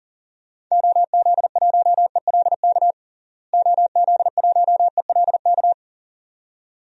Morse code, heard as a keyed tone, about 40 wpm, 700 Hz. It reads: OZ1ELK OZ1ELK
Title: DXCC-100 Call Signs - 6 Characters - Repeated 2x - 40wpm